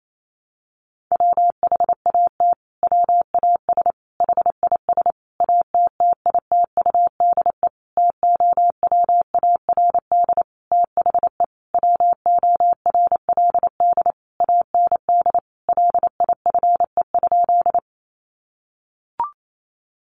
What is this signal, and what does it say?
Morse code, heard as a keyed tone, about 28 wpm, 700 Hz. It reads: WHAT WAS HIS ATTITUDE TOWARD THE WORLD AND LIFE?
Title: Sentences from Top 1300 Words 28wpm